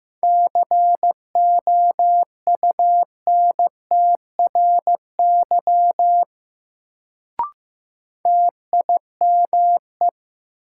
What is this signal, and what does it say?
Morse code, heard as a keyed tone, about 15 wpm, 700 Hz. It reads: COUNTRY TIME
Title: Sets of 2 Words - New Word Drill - 15wpm